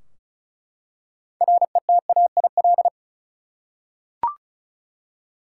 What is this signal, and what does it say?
Morse code, heard as a keyed tone, about 36 wpm, 700 Hz. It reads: RETAIL
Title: Practice QSO - Exchange 3 - More Info - 35wpm